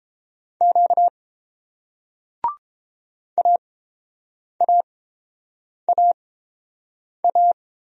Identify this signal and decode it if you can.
Morse code, heard as a keyed tone, about 30 wpm, spaced as for 7 wpm, 700 Hz. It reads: Q AAAA